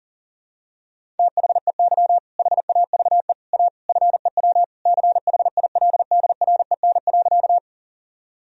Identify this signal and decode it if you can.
Morse code, heard as a keyed tone, about 40 wpm, 700 Hz. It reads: THEY HAVE A FEW CHILDREN.
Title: Sentences from Top 1000 Words - Review Part 1 - 40wpm